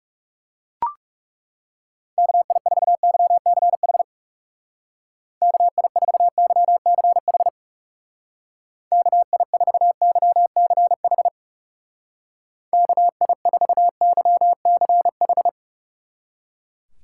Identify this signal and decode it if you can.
Morse code, heard as a keyed tone, about 36 wpm, 700 Hz. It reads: KI4YCH KI4YCH KI4YCH KI4YCH
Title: Speed Racing - USA 2x3 Call Signs 30wpm